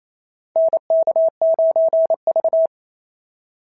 Morse code, heard as a keyed tone, about 27 wpm, 690 Hz. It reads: NK9V
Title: SOTA QSO Practice 28wpm